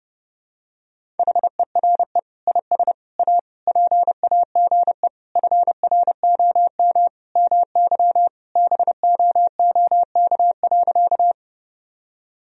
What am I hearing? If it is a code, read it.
Morse code, heard as a keyed tone, about 30 wpm, 700 Hz. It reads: HERE IS A PAGE FROM MY BOOK.
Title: Sentences from Top 1000 Words - Review Part 1 - 30wpm